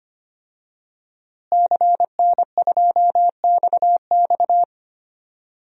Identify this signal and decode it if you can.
Morse code, heard as a keyed tone, about 25 wpm, 700 Hz. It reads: CN2XX
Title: DXCC-100 Call Signs - 5 Characters - Repeated 3x - 25wpm